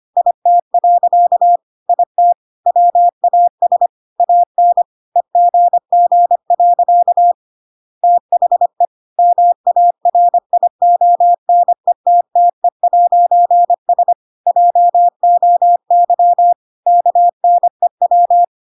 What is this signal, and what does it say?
Morse code, heard as a keyed tone, about 25 wpm, 700 Hz. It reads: IT. IT WAS AN EGG. THE MARIONETTE'S JOY KNEW
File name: Pinocchio / Pinocchio ch5 25 25 ik1hge